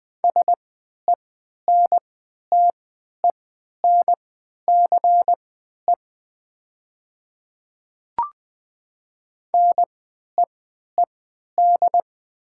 Morse code, heard as a keyed tone, about 20 wpm, spaced as for 11 wpm, 690 Hz. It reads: SENTENCE NEED
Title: 3x Character Spacing - Top 200 Words - 20wpm